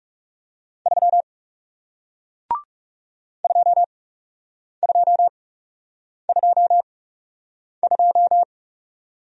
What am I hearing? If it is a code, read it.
Morse code, heard as a keyed tone, about 40 wpm, spaced as for 8 wpm, 700 Hz. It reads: 3 2222